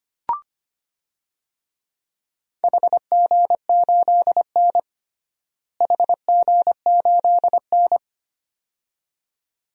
Morse code, heard as a keyed tone, about 25 wpm, 700 Hz. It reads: HG8N HG8N
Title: DXCC-100 Call Signs - 4 Characters - Repeated 2x - 25wpm